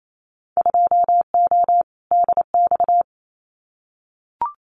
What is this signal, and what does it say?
Morse code, heard as a keyed tone, about 28 wpm, 700 Hz. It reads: 2O DX
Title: Winter Field Day Exchanges Encore 28wpm